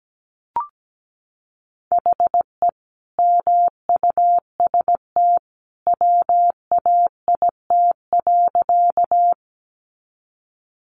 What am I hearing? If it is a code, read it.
Morse code, heard as a keyed tone, about 17 wpm, 700 Hz. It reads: HE MUST WAIT.
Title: Sentences from Top 500 Words 17wpm